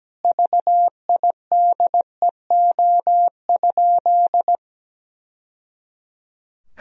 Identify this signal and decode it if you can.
Morse code, heard as a keyed tone, about 17 wpm, 700 Hz. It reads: VIDEO?
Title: Sentences from Top 1000 Words - Review Part 4 - 17wpm